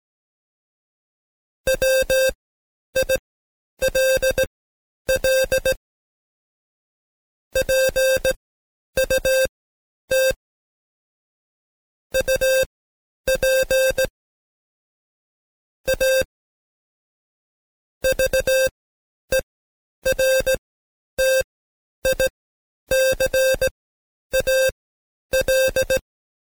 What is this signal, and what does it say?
Morse code, heard as a keyed tone, about 17 wpm, spaced as for 10 wpm, 530 Hz. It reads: WILL PUT UP A VERTICAL
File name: Code Quick / AudioDownld2 / Seg131-Stepper4 9WPM